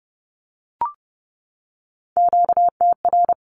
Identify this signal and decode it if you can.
Morse code, heard as a keyed tone, about 30 wpm, 700 Hz. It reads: QTR
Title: Q-Codes 30wpm